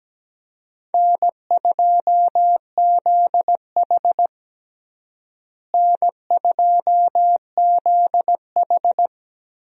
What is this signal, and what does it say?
Morse code, heard as a keyed tone, about 17 wpm, 700 Hz. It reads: N2ZH N2ZH
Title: USA Call Signs - 4 Characters - Repeated 3x - 17wpm